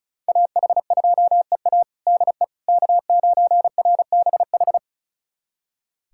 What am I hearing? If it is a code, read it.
Morse code, heard as a keyed tone, about 34 wpm, 700 Hz. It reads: AH2EA DE K9RBH
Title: USA Callsign DE Callsign - 5 Characters - 35wpm